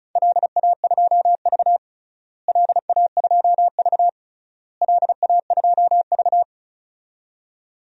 Morse code, heard as a keyed tone, about 36 wpm, 700 Hz. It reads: LA2V LA2V LA2V
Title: DXCC-100 Call Signs - 4 Characters - Repeated 3x - 35wpm